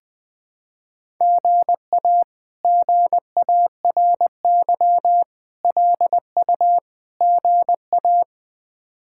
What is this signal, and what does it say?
Morse code, heard as a keyed tone, about 21 wpm, 700 Hz. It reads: GA GARY LU GA
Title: K1USN SST QSO Practice - 20wpm